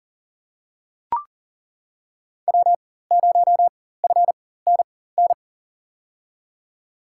Morse code, heard as a keyed tone, about 40 wpm, spaced as for 19 wpm, 700 Hz. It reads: W0FNN
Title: USA Call Signs - 5 Characters - 4x Character Spacing - 40wpm